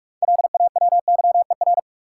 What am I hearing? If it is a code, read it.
Morse code, heard as a keyed tone, about 46 wpm, 700 Hz. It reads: LAWYER
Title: Six Letter Words - 45wpm